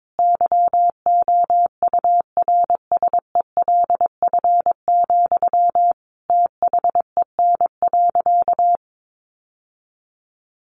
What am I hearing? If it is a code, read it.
Morse code, heard as a keyed tone, about 22 wpm, 700 Hz. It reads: YOURSELF, THEN.
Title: Sentences from Top 1100 Words 22wpm